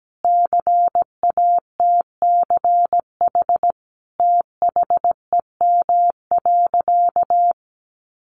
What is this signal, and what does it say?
Morse code, heard as a keyed tone, about 17 wpm, 700 Hz. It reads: CATCH THEM.
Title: Sentences from Top 1500 Words 17wpm